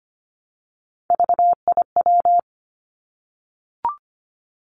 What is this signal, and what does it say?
Morse code, heard as a keyed tone, about 25 wpm, 700 Hz. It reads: VIW
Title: Three Challenging Letter-Number Combinations 25wpm